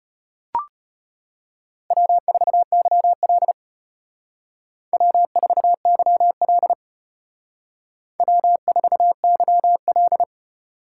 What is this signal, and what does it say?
Morse code, heard as a keyed tone, about 32 wpm, 700 Hz. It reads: W4YL W4YL W4YL